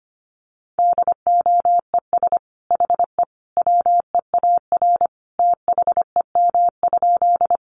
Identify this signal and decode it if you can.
Morse code, heard as a keyed tone, about 25 wpm, 700 Hz. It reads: DOES HE WEAR THEM?
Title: Sentences from Top 1500 Words 25wpm